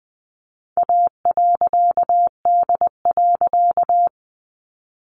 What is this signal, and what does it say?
Morse code, heard as a keyed tone, about 20 wpm, 700 Hz. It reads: A.D.
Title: A Sound Of Thunder 20wpm